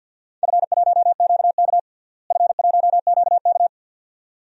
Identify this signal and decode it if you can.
Morse code, heard as a keyed tone, about 50 wpm, 700 Hz. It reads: F1QK F1QK
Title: DXCC-100 Call Signs - 4 Characters - Repeated 2x - 50wpm